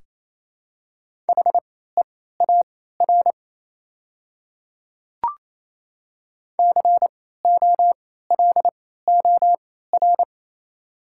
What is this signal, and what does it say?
Morse code, heard as a keyed tone, about 28 wpm, spaced as for 16 wpm, 700 Hz. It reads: HEAR COLOR
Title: 3x Character Spacing - Top 300 Words - 28wpm